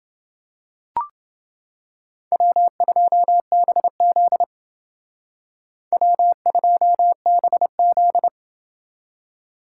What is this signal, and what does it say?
Morse code, heard as a keyed tone, about 28 wpm, 700 Hz. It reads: W2BZ W2BZ